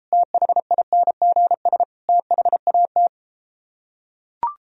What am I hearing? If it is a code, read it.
Morse code, heard as a keyed tone, about 33 wpm, 700 Hz. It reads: THINGS THAT